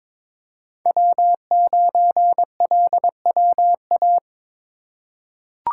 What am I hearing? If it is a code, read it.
Morse code, heard as a keyed tone, about 22 wpm, 700 Hz. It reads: W9LWA
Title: USA Call Signs - 5 Characters - 2x Character Spacing - 22wpm